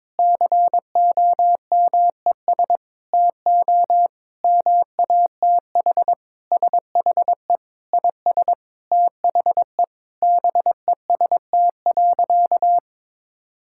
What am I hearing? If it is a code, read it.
Morse code, heard as a keyed tone, about 22 wpm, 700 Hz. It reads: COMES TO MATH SHE IS THE BEST.
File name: practice / 22wpm / Sentences with 3-Word Phrases - 22wpm